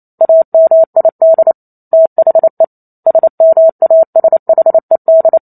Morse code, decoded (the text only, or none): AMID THE SMASHED